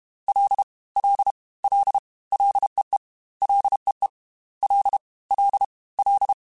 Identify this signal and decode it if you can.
Morse code, heard as a keyed tone, about 32 wpm, 800 Hz. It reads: L L L LEE LEE L L L
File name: Ls / Single L m3s / Single L Prefixe 3x2 Twice @30